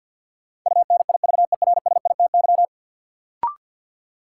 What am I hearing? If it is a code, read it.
Morse code, heard as a keyed tone, about 51 wpm, 700 Hz. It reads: UNIVERSITY
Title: Sentences from Top 1000 Words - Review Part 4 - New Word Drill - 50wpm